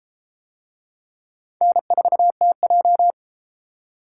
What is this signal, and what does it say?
Morse code, heard as a keyed tone, about 31 wpm, 700 Hz. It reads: N4TJ